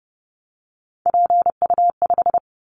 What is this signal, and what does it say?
Morse code, heard as a keyed tone, about 30 wpm, 700 Hz. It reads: PU5
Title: Callsign Prefixes for Top 50 DXCC - 30wpm